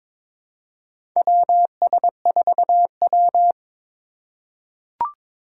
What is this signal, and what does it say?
Morse code, heard as a keyed tone, about 21 wpm, 700 Hz. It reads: WS4W
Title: USA Call Signs - 4 Characters - 8x Character Spacing - 22wpm